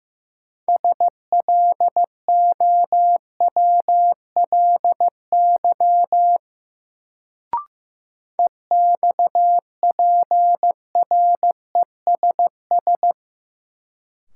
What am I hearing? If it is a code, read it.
Morse code, heard as a keyed tone, about 15 wpm, 700 Hz. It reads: SLOWLY EXPRESS
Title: Sentences from Top 1100 Words - New Word Drill Alt - 15wpm